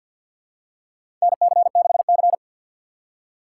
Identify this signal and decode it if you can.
Morse code, heard as a keyed tone, about 52 wpm, 700 Hz. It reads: NC6C
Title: USA Call Signs - 4 Characters - 3x Character Spacing - 50wpm